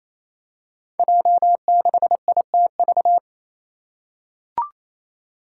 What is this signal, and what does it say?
Morse code, heard as a keyed tone, about 28 wpm, 700 Hz. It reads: J6ITV
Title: Five Challenging Letter-Number Combinations 28wpm